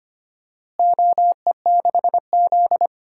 Morse code, heard as a keyed tone, about 25 wpm, 700 Hz. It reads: OE6Z
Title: DXCC-100 Call Signs - 4 Characters - Repeated 3x - 25wpm